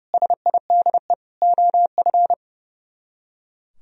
Morse code, heard as a keyed tone, about 35 wpm, 700 Hz. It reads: SIDE OF
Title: Speed-Racing - Sets of 4 Words 30wpm